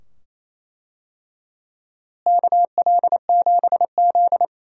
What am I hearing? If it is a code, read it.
Morse code, heard as a keyed tone, about 28 wpm, 700 Hz. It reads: KL7Z
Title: DXCC-100 Call Signs - 4 Characters - Repeated 2x - 28wpm